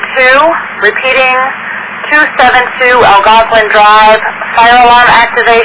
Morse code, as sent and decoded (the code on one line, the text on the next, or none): none